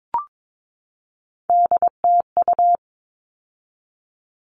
Morse code, heard as a keyed tone, about 22 wpm, 700 Hz. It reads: DTU
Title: All Cut Numbers - 3-Digit 22wpm